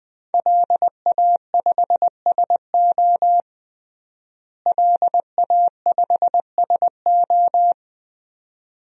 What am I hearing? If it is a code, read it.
Morse code, heard as a keyed tone, about 20 wpm, 700 Hz. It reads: LA5SO LA5SO
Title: DXCC-100 Call Signs - 5 Characters - Repeated 2x - 20wpm